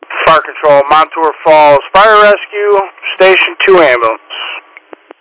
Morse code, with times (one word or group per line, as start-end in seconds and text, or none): none